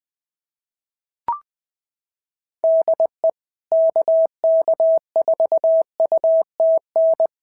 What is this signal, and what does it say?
Morse code, heard as a keyed tone, about 20 wpm, 690 Hz. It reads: DE KK4UTN
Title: SOTA QSO Practice 20wpm